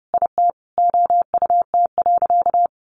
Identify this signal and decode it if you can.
Morse code, heard as a keyed tone, about 30 wpm, 700 Hz. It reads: IT OUT.
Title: Sentences from Top 1400 Words 30wpm